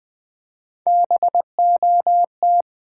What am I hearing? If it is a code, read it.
Morse code, heard as a keyed tone, about 20 wpm, 700 Hz. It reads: BOT
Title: Build A Word - 3 Letters At A Time - 6-Letter Words - 20wpm